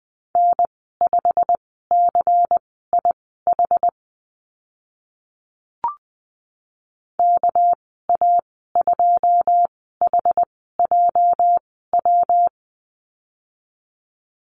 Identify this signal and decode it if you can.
Morse code, heard as a keyed tone, about 20 wpm, spaced as for 15 wpm, 700 Hz. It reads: N5CIH KA2HJW
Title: 207 Lesson 33g - Letter j - Call Signs 20wpm